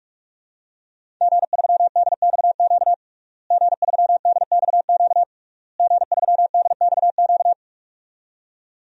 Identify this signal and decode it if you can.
Morse code, heard as a keyed tone, about 46 wpm, 700 Hz. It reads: G3DXQ G3DXQ G3DXQ
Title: DXCC-100 Call Signs - 5 Characters - Repeated 3x - 45wpm